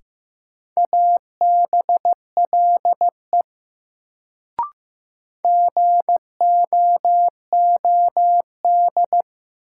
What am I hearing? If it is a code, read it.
Morse code, heard as a keyed tone, about 15 wpm, 700 Hz. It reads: ABLE GOOD
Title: Sets of 2 Words - New Word Drill - 15wpm